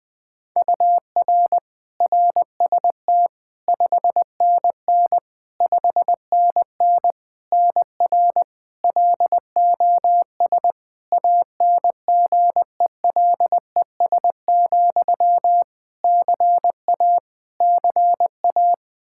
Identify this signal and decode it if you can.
Morse code, heard as a keyed tone, about 20 wpm, 700 Hz. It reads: UR RST 5NN 5NN NR LOS ANGELES, CA CA
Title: K1USN SST QSO Practice 20wpm